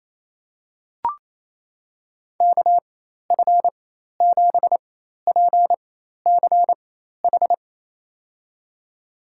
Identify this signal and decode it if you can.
Morse code, heard as a keyed tone, about 28 wpm, spaced as for 13 wpm, 700 Hz. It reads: KF7PCH
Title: USA Call Signs - 4x Character Spacing - 28wpm